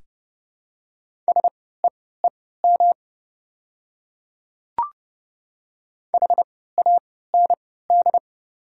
Morse code, heard as a keyed tone, about 30 wpm, spaced as for 17 wpm, 690 Hz. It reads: SEEM HAND
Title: 3x Character Spacing - Top 300 Words - 30wpm